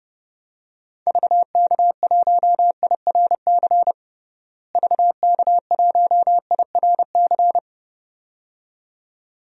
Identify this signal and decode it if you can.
Morse code, heard as a keyed tone, about 30 wpm, 700 Hz. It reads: VK1IRC VK1IRC
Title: DXCC-100 Call Signs - 6 Characters - Repeated 2x - 30wpm